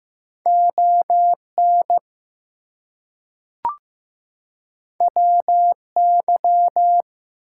Morse code, heard as a keyed tone, about 15 wpm, 700 Hz. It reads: ON WY
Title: US States and Canadian Provinces 15wpm